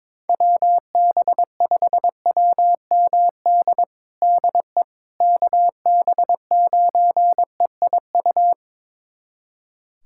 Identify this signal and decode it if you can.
Morse code, heard as a keyed tone, about 22 wpm, 700 Hz. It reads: WB5WMD DE KB9EIU
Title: USA Callsign DE Callsign - 6 Characters - 22wpm